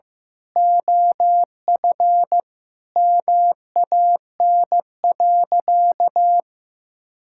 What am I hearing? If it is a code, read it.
Morse code, heard as a keyed tone, about 15 wpm, 700 Hz. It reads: OF MAN.
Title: Sentences from Top 1000 Words - Review Part 5 - 15wpm